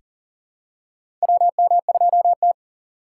Morse code, heard as a keyed tone, about 38 wpm, 700 Hz. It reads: WM2T
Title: USA Call Signs - 4 Characters - 4x Character Spacing - 40wpm